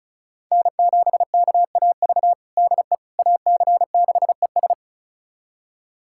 Morse code, heard as a keyed tone, about 36 wpm, 700 Hz. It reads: N7KAV DE AC6ES
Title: USA Callsign DE Callsign - 5 Characters - 35wpm